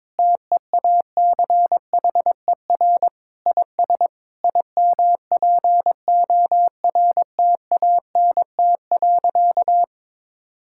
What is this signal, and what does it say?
Morse code, heard as a keyed tone, about 22 wpm, 700 Hz. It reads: TEACHER IS IMPORTANT.
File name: practice / 22wpm / Sentences with 3-Word Phrases - 22wpm